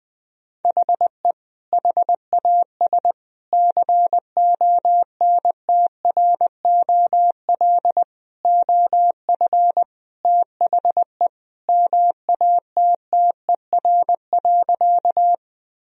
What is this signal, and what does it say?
Morse code, heard as a keyed tone, about 20 wpm, 700 Hz. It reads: HE HAS CONTROL OF THE MATTER.
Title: Sentences from Top 1000 Words - Review Part 3 - 20wpm